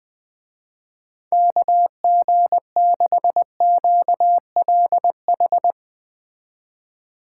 Morse code, heard as a keyed tone, about 20 wpm, 700 Hz. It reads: KG6QLH
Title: Speed Racing - USA Call Signs 17wpm